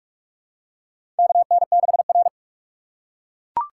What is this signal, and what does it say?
Morse code, heard as a keyed tone, about 44 wpm, 700 Hz. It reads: KN6R
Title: USA Call Signs - 4 Characters - 4x Character Spacing - 45wpm